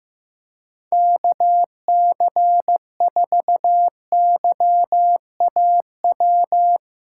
Morse code, heard as a keyed tone, about 15 wpm, 700 Hz. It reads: KC4YAW
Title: USA Callsign DE Callsign 15wpm